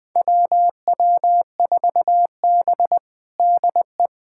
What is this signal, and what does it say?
Morse code, heard as a keyed tone, about 20 wpm, 700 Hz. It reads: WW4B DE
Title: USA Callsign DE Callsign - 4 Characters - 20wpm